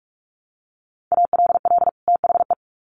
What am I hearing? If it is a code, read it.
Morse code, heard as a keyed tone, about 44 wpm, 700 Hz. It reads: ALL THE
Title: Sets of 2 Words 45wpm